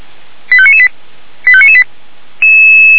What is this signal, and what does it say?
Five-tone selective call (ZVEI), heard as quick runs of five tones, tones of about 70 ms. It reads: 85998 85998